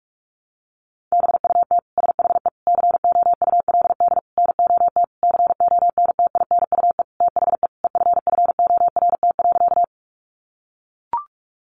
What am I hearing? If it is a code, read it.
Morse code, heard as a keyed tone, about 44 wpm, 700 Hz. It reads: BUT SHE COULD NOT CONTINUE THE EFFORT.
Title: Sentences from Top 1200 Words 45wpm